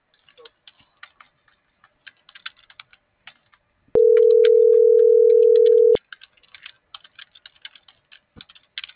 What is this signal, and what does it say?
Ringback tone (North America), 440 + 480 Hz.